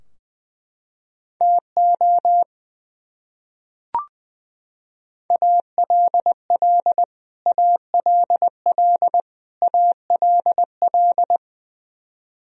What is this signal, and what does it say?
Morse code, heard as a keyed tone, about 21 wpm, 700 Hz. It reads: TO ALL ALL ALL
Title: Top 100 Words Repeated 20wpm